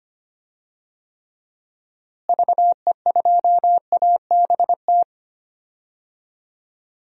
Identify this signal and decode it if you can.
Morse code, heard as a keyed tone, about 25 wpm, 700 Hz. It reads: VE2ABT
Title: DXCC-100 Call Signs - 6 Characters - Repeated 3x - 25wpm